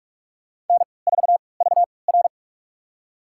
Morse code, heard as a keyed tone, about 43 wpm, spaced as for 25 wpm, 690 Hz. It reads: N4VR
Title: USA Call Signs - 4 Characters - 3x Character Spacing - 45wpm